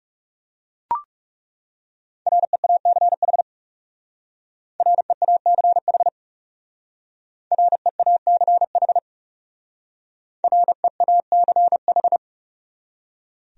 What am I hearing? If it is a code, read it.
Morse code, heard as a keyed tone, about 38 wpm, 700 Hz. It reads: REACH REACH REACH REACH
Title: Speed-Racing Top 500 Words 30wpm